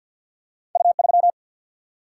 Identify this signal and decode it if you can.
Morse code, heard as a keyed tone, about 50 wpm, 700 Hz. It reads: U3
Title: Single-Letter Single-Number 50wpm